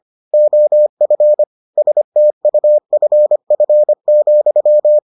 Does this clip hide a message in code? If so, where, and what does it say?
Yes: Morse code from 0.3 to 5.0 s, heard as a keyed tone, about 25 wpm, 600 Hz. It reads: OF STUFF,